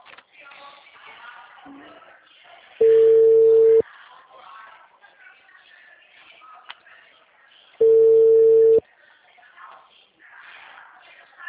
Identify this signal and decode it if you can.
Ringback tone (China, 450 Hz), with speech.
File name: out- -2018- - - .4110